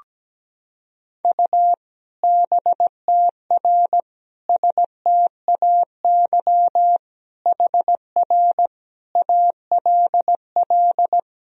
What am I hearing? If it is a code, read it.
Morse code, heard as a keyed tone, about 17 wpm, 700 Hz. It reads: U BTR STAY HR ALL
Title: Sentences from Top 500 Words and Abbreviations 17wpm